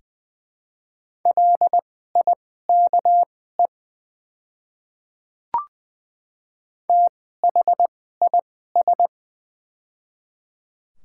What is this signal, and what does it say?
Morse code, heard as a keyed tone, about 20 wpm, spaced as for 14 wpm, 690 Hz. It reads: LIKE THIS
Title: 2x Character Spacing - Top 100 Words - 20wpm